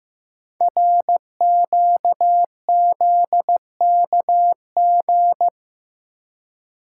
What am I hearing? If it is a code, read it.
Morse code, heard as a keyed tone, about 15 wpm, 700 Hz. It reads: RQZKG
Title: Code Groups - 5 Letter:Numbers - 15wpm